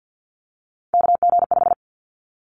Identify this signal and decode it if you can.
Morse code, heard as a keyed tone, about 50 wpm, 700 Hz. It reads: KG5
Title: Callsign Prefixes for Top 25 DXCC - Repeated 3x - 50wpm